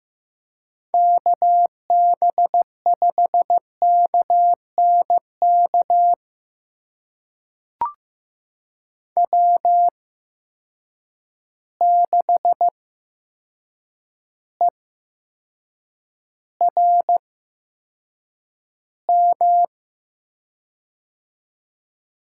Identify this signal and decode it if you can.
Morse code, heard as a keyed tone, about 15 wpm, spaced as for 7 wpm, 700 Hz. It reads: KB5KNK W6ERM
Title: USA Call Signs - 8x Character Spacing - 15wpm